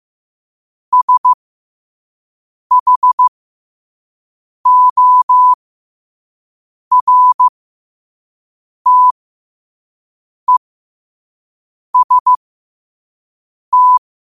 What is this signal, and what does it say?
Morse code, heard as a keyed tone, about 15 wpm, spaced as for 5 wpm, 1000 Hz. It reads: SHORTEST
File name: qotdfiles / 5wpm